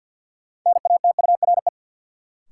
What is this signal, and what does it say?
Morse code, heard as a keyed tone, about 52 wpm, 700 Hz. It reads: NATURE
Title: Sets of 3 Words - New Word Drill - 50wpm